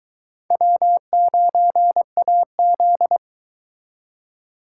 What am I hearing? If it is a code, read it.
Morse code, heard as a keyed tone, about 22 wpm, 700 Hz. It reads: W9AZ